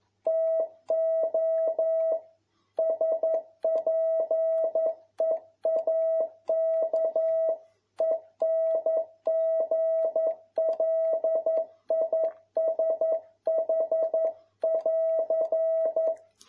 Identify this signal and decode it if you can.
Morse code, heard as a keyed tone, about 11 wpm, 650 Hz. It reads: TO SPEAK ENGLISH<AR>
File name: jh8cbh / 2021cw